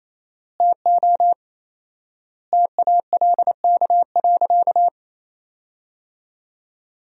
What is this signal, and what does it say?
Morse code, heard as a keyed tone, about 28 wpm, 700 Hz. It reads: TO TALK.
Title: 4x Word-Spacing - Sentences from Top 400 Words - 28wpm